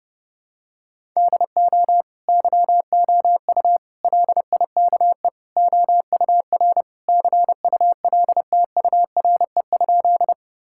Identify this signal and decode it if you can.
Morse code, heard as a keyed tone, about 30 wpm, 700 Hz. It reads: DO YOU LIKE OUR CULTURE?
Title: Sentences from Top 1000 Words - Review Part 6 - 30wpm